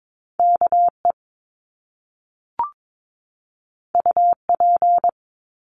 Morse code, heard as a keyed tone, about 22 wpm, 700 Hz. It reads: KE UP
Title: Top Two Letter Combinations in Words 22wpm